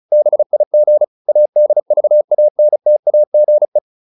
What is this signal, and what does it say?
Morse code, heard as a keyed tone, about 34 wpm, 600 Hz. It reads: BIG ADVANTAGE